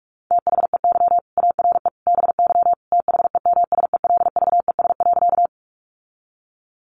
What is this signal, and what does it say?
Morse code, heard as a keyed tone, about 44 wpm, 700 Hz. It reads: THEY ARE BY THEMSELVES.
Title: Sentences from Top 800 Words 45wpm